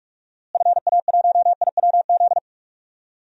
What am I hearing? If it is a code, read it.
Morse code, heard as a keyed tone, about 46 wpm, 700 Hz. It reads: FA1IWZ
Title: DXCC-100 Call Signs - 6 Characters - Repeated 3x - 45wpm